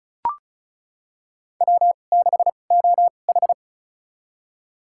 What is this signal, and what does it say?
Morse code, heard as a keyed tone, about 36 wpm, spaced as for 25 wpm, 700 Hz. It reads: W6OH
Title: USA Call Signs - 2x Character Spacing - 35wpm